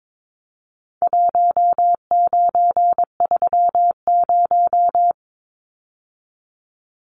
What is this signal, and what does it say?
Morse code, heard as a keyed tone, about 22 wpm, 700 Hz. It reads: 1930
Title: Numbers - Contest Style - 4-Digit 22wpm